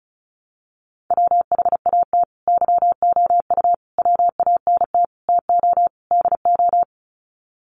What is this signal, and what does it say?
Morse code, heard as a keyed tone, about 34 wpm, 700 Hz. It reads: WHAT YOU WANT TO DO
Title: Sets of 5 Words Encore 35wpm